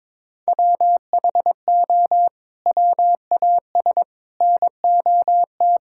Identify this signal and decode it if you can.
Morse code, heard as a keyed tone, about 22 wpm, 700 Hz. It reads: WHO WAS NOT
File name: practice / 22wpm / Sets of 6 Words - 22wpm